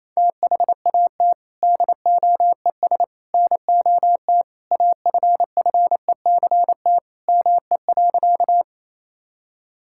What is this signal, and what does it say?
Morse code, heard as a keyed tone, about 28 wpm, 700 Hz. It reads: THAT DOES NOT AFFECT ME.
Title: Sentences from Top 1000 Words - Review Part 6 - 28wpm